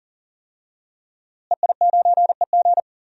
Morse code, heard as a keyed tone, about 40 wpm, 700 Hz. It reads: EI9EG
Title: DXCC-100 Call Signs - 5 Characters - Repeated 3x - 40wpm